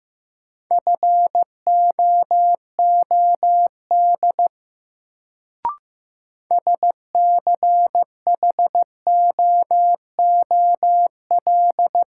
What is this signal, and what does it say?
Morse code, heard as a keyed tone, about 15 wpm, 700 Hz. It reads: FOOD SCHOOL
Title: Sentences from Top 300 Words - New Word Drill - 15wpm